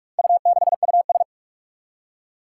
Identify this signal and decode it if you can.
Morse code, heard as a keyed tone, about 43 wpm, 700 Hz. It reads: U6US
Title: Four Challenging Letter-Number Combinations 45wpm